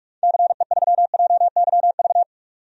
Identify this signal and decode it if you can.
Morse code, heard as a keyed tone, about 46 wpm, 700 Hz. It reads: CE3JYV
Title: DXCC-100 Call Signs - 6 Characters - 45wpm